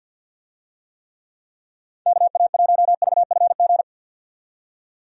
Morse code, heard as a keyed tone, about 51 wpm, 700 Hz. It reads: KA1VFG
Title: USA Call Signs - 6 Characters - 50wpm